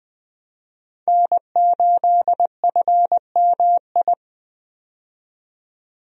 Morse code, heard as a keyed tone, about 20 wpm, 700 Hz. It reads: N8FMI
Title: USA Call Signs - 5 Characters - 20wpm